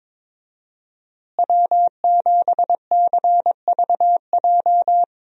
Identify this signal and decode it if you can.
Morse code, heard as a keyed tone, about 22 wpm, 700 Hz. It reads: W7CVJ
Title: USA Call Signs - 5 Characters - Repeated 2x - 22wpm